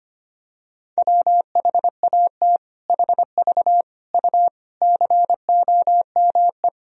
Morse code, heard as a keyed tone, about 25 wpm, 700 Hz. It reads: WHAT HV U COME
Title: Sentences from Top 100 Words and Abbreviations 25wpm